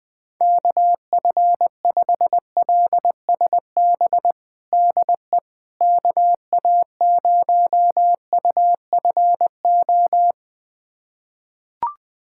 Morse code, heard as a keyed tone, about 21 wpm, 700 Hz. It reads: KF5LSB DE KA0UFO
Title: USA Callsign DE Callsign 20wpm